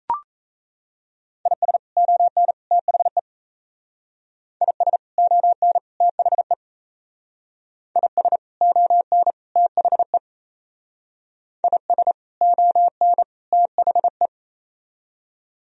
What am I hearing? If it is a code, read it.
Morse code, heard as a keyed tone, about 33 wpm, 700 Hz. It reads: IS ON THE IS ON THE IS ON THE IS ON THE
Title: Speed-Racing - Sets of 3 Words 28wpm